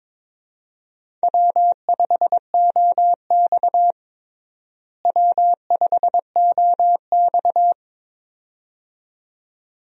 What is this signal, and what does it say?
Morse code, heard as a keyed tone, about 22 wpm, 700 Hz. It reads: W5OX W5OX
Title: USA Call Signs - 4 Characters - Repeated 2x - 22wpm